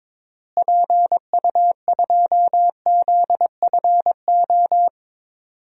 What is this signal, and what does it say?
Morse code, heard as a keyed tone, about 22 wpm, 700 Hz. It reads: PU2ZFO
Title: DXCC-100 Call Signs - 6 Characters - Repeated 2x - 22wpm